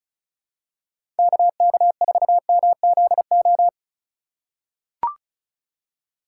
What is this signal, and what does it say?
Morse code, heard as a keyed tone, about 35 wpm, 690 Hz. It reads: KK4MZO
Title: USA Call Signs - 6 Characters - 6x Character Spacing - 35wpm